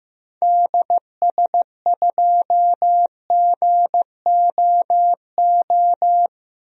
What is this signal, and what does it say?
Morse code, heard as a keyed tone, about 15 wpm, 700 Hz. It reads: DS2GOO
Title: DXCC-100 Call Signs - 6 Characters - Repeated 3x - 15wpm